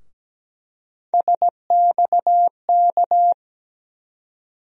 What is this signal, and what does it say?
Morse code, heard as a keyed tone, about 17 wpm, 700 Hz. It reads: SXK
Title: USA Call Sign Postfixes - 3 Letters 17wpm